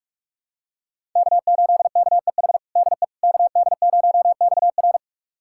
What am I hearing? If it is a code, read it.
Morse code, heard as a keyed tone, about 41 wpm, 700 Hz. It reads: K8KEH DE KD0XR
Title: USA Callsign DE Callsign - 5 Characters - 45wpm